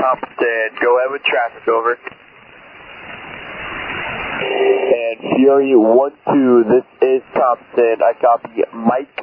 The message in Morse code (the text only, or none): none